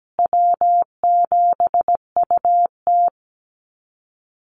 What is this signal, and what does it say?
Morse code, heard as a keyed tone, about 17 wpm, 700 Hz. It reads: W7UT
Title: Easy Call Signs Repeated 2x - 17wpm